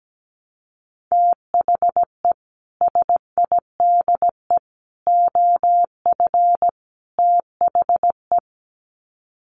Morse code, heard as a keyed tone, about 17 wpm, 700 Hz. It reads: THE SIDE OF THE
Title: Sets of 4 Words 17wpm